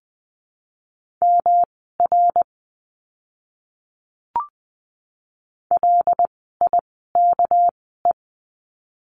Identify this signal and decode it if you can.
Morse code, heard as a keyed tone, about 20 wpm, spaced as for 15 wpm, 700 Hz. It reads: MR LIKE
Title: 244 Lesson 39f - Letter z - Words 20wpm